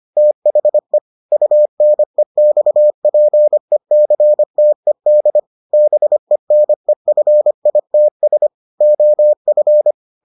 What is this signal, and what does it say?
Morse code, heard as a keyed tone, about 25 wpm, 600 Hz. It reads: THE UNEXPECTED BENEFITS OF